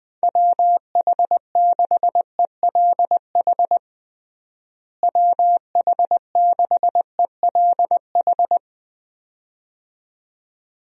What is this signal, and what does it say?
Morse code, heard as a keyed tone, about 20 wpm, 700 Hz. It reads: WH6ELH WH6ELH
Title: USA Call Signs - 6 Characters - Repeated 2x - 20wpm